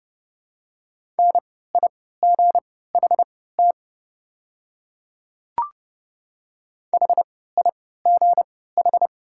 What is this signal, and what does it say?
Morse code, heard as a keyed tone, about 31 wpm, spaced as for 17 wpm, 700 Hz. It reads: NIGHT HIGH
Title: 3x Character Spacing - Top 300 Words - 30wpm